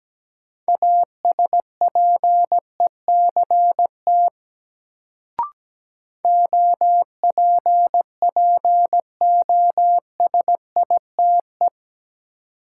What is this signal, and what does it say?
Morse code, heard as a keyed tone, about 17 wpm, 700 Hz. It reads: ASPECT OPPOSITE
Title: Sentences from Top 1300 Words - New Word Drill - 17wpm